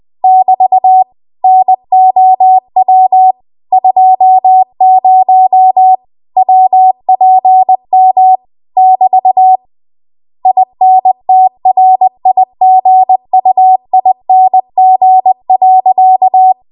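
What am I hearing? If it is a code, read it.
Morse code, heard as a keyed tone, about 20 wpm, 750 Hz. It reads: <BT> NOW 20 WPM <BT> INTRIGUING.
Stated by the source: Morse code practice at 20 WPM.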